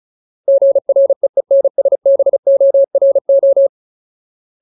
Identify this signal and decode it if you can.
Morse code, heard as a keyed tone, about 34 wpm, 550 Hz. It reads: GREENSBORO
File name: Ultimate Word List / 35WPM Ultimate